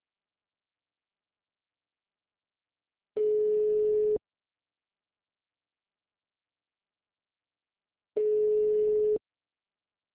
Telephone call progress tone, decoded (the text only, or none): ringback tone